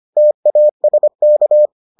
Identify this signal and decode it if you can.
Morse code, heard as a keyed tone, about 25 wpm, 600 Hz. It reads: TASK